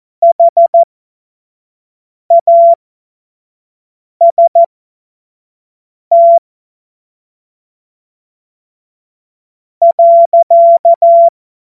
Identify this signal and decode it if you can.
Morse code, heard as a keyed tone, about 14 wpm, spaced as for 4 wpm, 680 Hz. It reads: HAST.
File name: Morsekurs / Witze / Witz 08. Februar 2022 25 F70